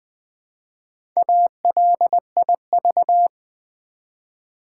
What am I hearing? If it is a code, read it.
Morse code, heard as a keyed tone, about 20 wpm, 700 Hz. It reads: ALIV
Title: Build A Word - 1 Letter At A Time - 5-Letter Words - 20wpm